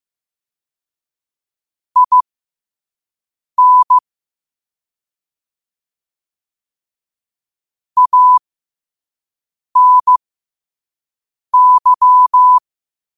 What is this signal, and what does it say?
Morse code, heard as a keyed tone, about 15 wpm, spaced as for 5 wpm, 1000 Hz. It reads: IN ANY
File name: qotdfiles / 5wpm